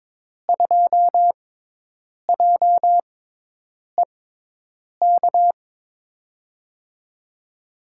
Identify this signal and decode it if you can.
Morse code, heard as a keyed tone, about 22 wpm, spaced as for 8 wpm, 700 Hz. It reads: 2JEK
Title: USA Call Signs - 5 Characters - 6x Character Spacing - 22wpm